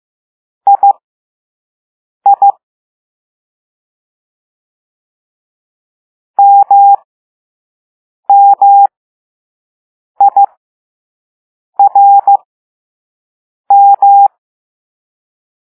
Morse code, heard as a keyed tone, about 15 wpm, spaced as for 5 wpm, 800 Hz. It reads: II MMIRM
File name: Morse / Level2B